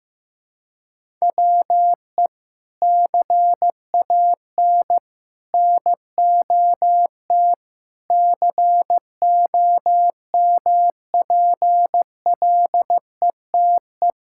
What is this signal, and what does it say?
Morse code, heard as a keyed tone, about 16 wpm, 700 Hz. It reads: WE CAN NOT COMPLETE
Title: Sentences from Top 1000 Words - Review Part 6 - 15wpm